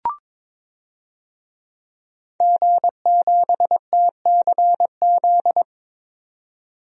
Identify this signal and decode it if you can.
Morse code, heard as a keyed tone, about 22 wpm, 700 Hz. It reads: G7TCZ
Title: DXCC-100 Call Signs - 5 Characters - 22wpm